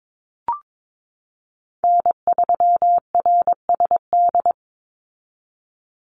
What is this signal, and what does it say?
Morse code, heard as a keyed tone, about 22 wpm, 700 Hz. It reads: N3RSD
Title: Call Signs Repeated 3x - 22wpm